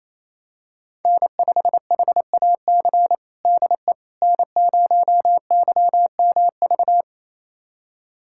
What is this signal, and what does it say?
Morse code, heard as a keyed tone, about 28 wpm, 700 Hz. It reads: N5HAC DE N0YMV
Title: USA Callsign DE Callsign - 5 Characters - 28wpm